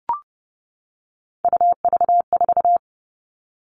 Morse code, heard as a keyed tone, about 30 wpm, 700 Hz. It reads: UV4
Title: All Cut Numbers - 3-Digit 30wpm